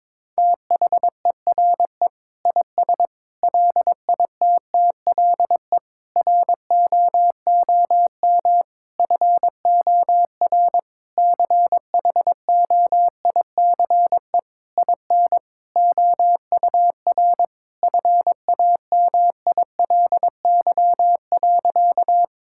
Morse code, heard as a keyed tone, about 21 wpm, 700 Hz. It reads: THERE IS LITTLE ROOM FOR CHOICE IN OUR FAMILY.
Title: Sentences from Top 1000 Words - Review Part 6 - 22wpm